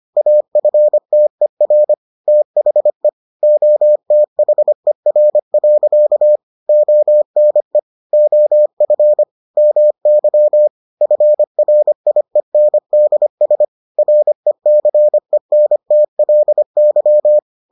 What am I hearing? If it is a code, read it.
Morse code, heard as a keyed tone, about 25 wpm, 600 Hz. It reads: AFTER THE OTHER. ONE OF MY FRIENDS RECENTLY